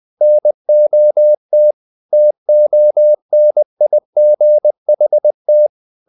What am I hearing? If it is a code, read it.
Morse code, heard as a keyed tone, about 20 wpm, 600 Hz. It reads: NOT TONIGHT
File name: the old man and the sea part 2 20wpm